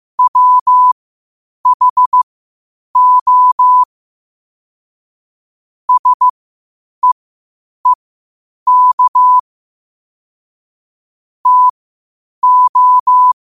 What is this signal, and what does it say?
Morse code, heard as a keyed tone, about 15 wpm, spaced as for 8 wpm, 1000 Hz. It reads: WHO SEEK TO
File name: qotdfiles / 8wpm